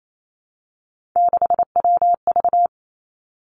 Morse code, heard as a keyed tone, about 28 wpm, 700 Hz. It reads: 6WV
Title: Three Challenging Letter-Number Combinations 28wpm